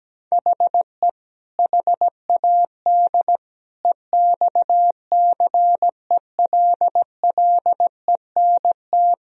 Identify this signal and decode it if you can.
Morse code, heard as a keyed tone, about 17 wpm, 700 Hz. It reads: HE HAD EXCELLENT
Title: Sentences from Top 1000 Words - Review Part 3 - 17wpm